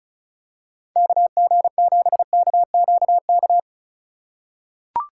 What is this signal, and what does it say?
Morse code, heard as a keyed tone, about 36 wpm, 700 Hz. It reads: KG7KQK
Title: USA Call Signs - 6 Characters - 3x Character Spacing - 35wpm